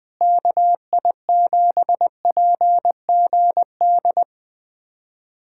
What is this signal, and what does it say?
Morse code, heard as a keyed tone, about 21 wpm, 700 Hz. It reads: KI7PGD
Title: USA Callsign DE Callsign 20wpm